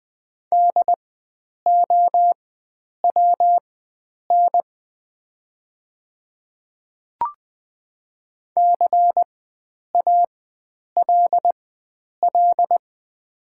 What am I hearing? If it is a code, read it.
Morse code, heard as a keyed tone, about 20 wpm, spaced as for 9 wpm, 700 Hz. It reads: DOWN CALL
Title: 4x Character Spacing - Top 100 Words - 20wpm